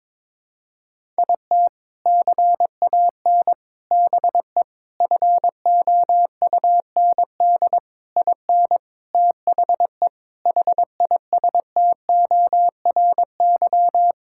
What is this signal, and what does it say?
Morse code, heard as a keyed tone, about 22 wpm, 700 Hz. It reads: IT CAN BE FOUND IN THE HISTORY
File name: practice / 22wpm / Sentences with 4-Word Phrases - 22wpm